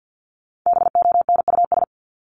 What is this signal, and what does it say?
Morse code, heard as a keyed tone, about 50 wpm, 700 Hz. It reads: BONUS
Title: Top 1751-2000 Words 50wpm